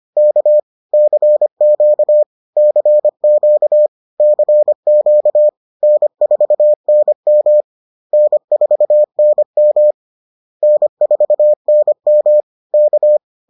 Morse code, heard as a keyed tone, about 25 wpm, 600 Hz. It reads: K CQ CQ CQ N4NM N4NM N4NM K